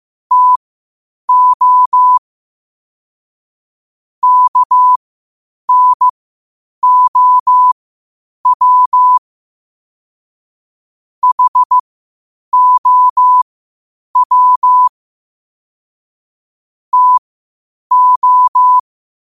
Morse code, heard as a keyed tone, about 15 wpm, spaced as for 8 wpm, 1000 Hz. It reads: TO KNOW HOW TO
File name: qotdfiles / 8wpm